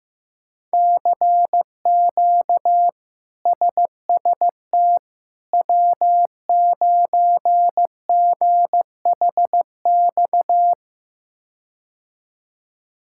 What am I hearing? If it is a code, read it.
Morse code, heard as a keyed tone, about 15 wpm, 700 Hz. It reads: CQ SST W9GHX
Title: K1USN SST QSO Practice - 15wpm